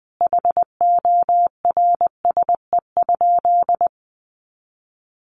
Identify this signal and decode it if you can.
Morse code, heard as a keyed tone, about 20 wpm, 700 Hz. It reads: HORSE?
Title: Sentences from Top 500 Words 20wpm